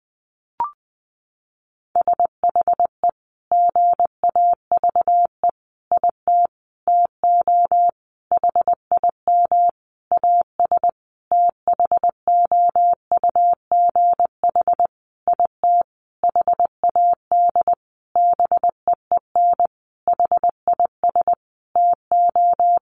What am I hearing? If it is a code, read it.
Morse code, heard as a keyed tone, about 20 wpm, 700 Hz. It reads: SHE GAVE IT TO HIM AS THOUGH IT HAD BEEN HIS TO
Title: Sentences from Top 1500 Words 20wpm